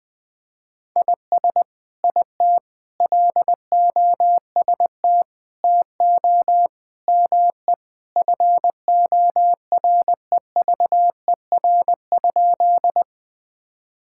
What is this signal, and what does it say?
Morse code, heard as a keyed tone, about 21 wpm, 700 Hz. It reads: IS IT LOST TO ME FOREVER?
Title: Sentences from Top 1600 Words 20wpm